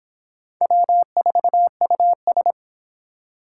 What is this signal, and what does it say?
Morse code, heard as a keyed tone, about 26 wpm, 700 Hz. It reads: W4US